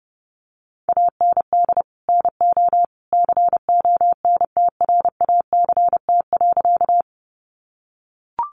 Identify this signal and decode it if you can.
Morse code, heard as a keyed tone, about 30 wpm, 700 Hz. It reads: AND NO CONTRACT.
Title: Sentences from Top 1400 Words 30wpm